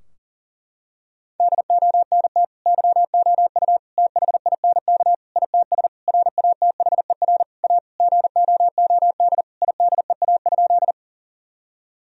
Morse code, heard as a keyed tone, about 40 wpm, 700 Hz. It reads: DONT YOU THINK ITS RATHER A GOOD IDEA?
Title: Sentences from Top 1000 Words - Review Part 4 - 40wpm